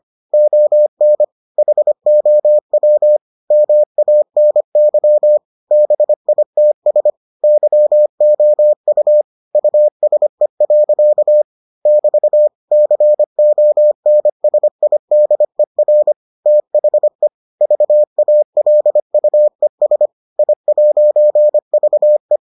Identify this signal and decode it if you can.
Morse code, heard as a keyed tone, about 25 wpm, 600 Hz. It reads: ON HOW MANY BITS YOU USE. <BT> CONSIDER THE VALUES I'VE